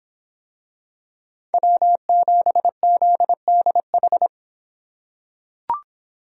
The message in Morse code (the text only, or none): W7ZDH